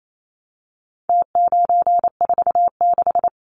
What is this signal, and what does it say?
Morse code, heard as a keyed tone, about 28 wpm, 700 Hz. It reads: T946
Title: Numbers - Contest Style - 4-Digit 28wpm